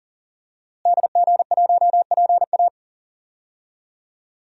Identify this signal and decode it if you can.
Morse code, heard as a keyed tone, about 37 wpm, 700 Hz. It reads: DG1PA
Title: DXCC-100 Call Signs - 5 Characters - Repeated 2x - 40wpm